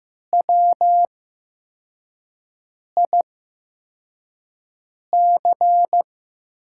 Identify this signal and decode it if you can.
Morse code, heard as a keyed tone, about 15 wpm, spaced as for 4 wpm, 700 Hz. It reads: WIC
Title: USA Call Signs - 5 Characters - 8x Character Spacing - 15wpm